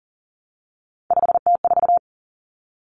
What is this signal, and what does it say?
Morse code, heard as a keyed tone, about 40 wpm, 700 Hz. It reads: 5T4
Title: Common Cut Numbers - 3-Digit 40wpm